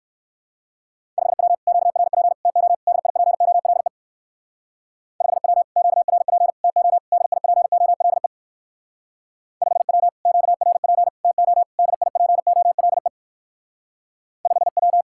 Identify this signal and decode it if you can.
Morse code, heard as a keyed tone, about 59 wpm, 700 Hz. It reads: 5W QRP TO DIPOLE 5W QRP TO DIPOLE 5W QRP TO DIPOLE 5W